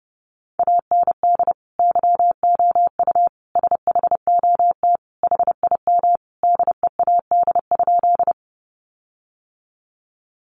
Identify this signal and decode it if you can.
Morse code, heard as a keyed tone, about 30 wpm, 700 Hz. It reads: AND YOU SHOT HIM DEAD?
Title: Sentences from Top 1500 Words 30wpm